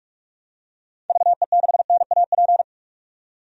Morse code, heard as a keyed tone, about 46 wpm, 700 Hz. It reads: VE6NAP
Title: DXCC-100 Call Signs - 6 Characters - 45wpm